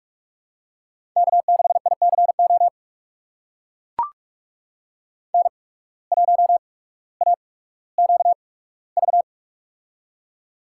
Morse code, heard as a keyed tone, about 44 wpm, spaced as for 19 wpm, 700 Hz. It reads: K6ICO N1AQV
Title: USA Call Signs - 5 Characters - 8x Character Spacing - 45wpm